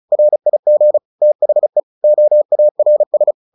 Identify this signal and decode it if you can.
Morse code, heard as a keyed tone, about 34 wpm, 600 Hz. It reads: RIG THE OARS
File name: the old man and the sea 2 35wpm